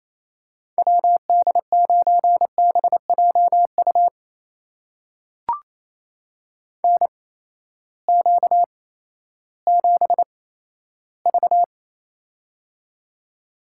Morse code, heard as a keyed tone, about 29 wpm, spaced as for 14 wpm, 700 Hz. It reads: WD9BJU NQ7V
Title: USA Call Signs - 8x Character Spacing - 28wpm